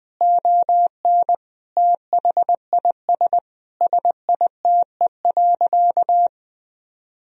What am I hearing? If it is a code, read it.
Morse code, heard as a keyed tone, about 20 wpm, 700 Hz. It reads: ON THIS SITE.
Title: Sentences from Top 1000 Words - Review Part 4 - 20wpm